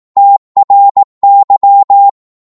Morse code, heard as a keyed tone, about 18 wpm, 800 Hz. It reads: TRY